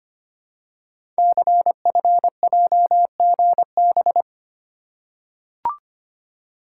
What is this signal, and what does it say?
Morse code, heard as a keyed tone, about 25 wpm, 700 Hz. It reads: CFJGB
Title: Code Groups - 5 Letters - 25wpm